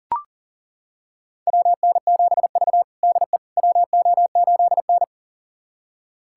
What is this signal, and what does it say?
Morse code, heard as a keyed tone, about 40 wpm, 700 Hz. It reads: WN7V DE WO8N
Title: USA Callsign DE Callsign - 4 Characters - 40wpm